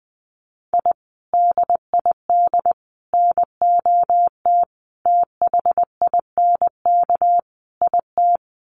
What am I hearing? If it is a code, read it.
Morse code, heard as a keyed tone, about 20 wpm, 700 Hz. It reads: I DID NOT THINK IT
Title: Sets of 5 Words Encore2 20wpm